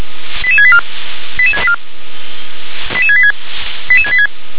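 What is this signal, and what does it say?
Five-tone selective call (ZVEI), heard as quick runs of five tones, tones of about 80 ms. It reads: 88684 88684 88686 88686
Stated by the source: ZVEI five-tone call received by SDR.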